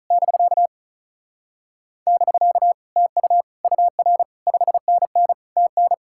Morse code, heard as a keyed tone, about 42 wpm, 700 Hz. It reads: <BK> <BK> TU UR 5NN TN
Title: Speed-Racing Common QSO Phrases Encore - 35wpm